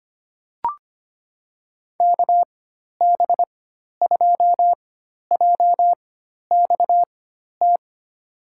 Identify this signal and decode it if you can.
Morse code, heard as a keyed tone, about 25 wpm, spaced as for 12 wpm, 700 Hz. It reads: KB2JXT
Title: USA Call Signs - 6 Characters - 4x Character Spacing - 25wpm